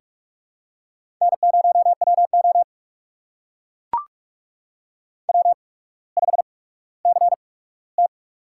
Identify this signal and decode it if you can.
Morse code, heard as a keyed tone, about 44 wpm, spaced as for 19 wpm, 700 Hz. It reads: N0WO W5CT
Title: USA Call Signs - 4 Characters - 8x Character Spacing - 45wpm